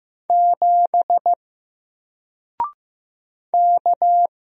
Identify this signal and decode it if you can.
Morse code, heard as a keyed tone, about 15 wpm, 700 Hz. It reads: 7 K
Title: Single Letter-Number Alt 15wpm